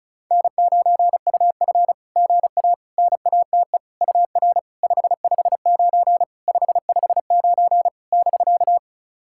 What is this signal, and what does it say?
Morse code, heard as a keyed tone, about 36 wpm, 690 Hz. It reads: N9UF GA NATE UR 559 559 <BK>
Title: SOTA QSO Practice 35wpm